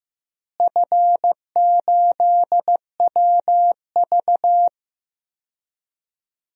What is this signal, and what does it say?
Morse code, heard as a keyed tone, about 15 wpm, 700 Hz. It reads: F8WV
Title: DXCC-100 Call Signs - 4 Characters - Repeated 3x - 15wpm